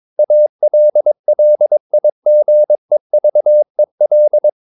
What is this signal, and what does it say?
Morse code, heard as a keyed tone, about 22 wpm, 600 Hz. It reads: ALLIGEVEL
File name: Ferie QRS 22wpm